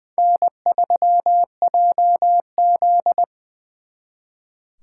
Morse code, heard as a keyed tone, about 24 wpm, 700 Hz. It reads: N3JZ
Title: Speed Racing - USA Call Signs 20wpm